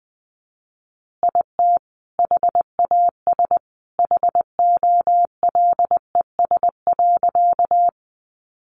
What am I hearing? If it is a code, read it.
Morse code, heard as a keyed tone, about 20 wpm, 700 Hz. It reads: IT HAS HOLES.
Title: Additional Easy Phrases 20wpm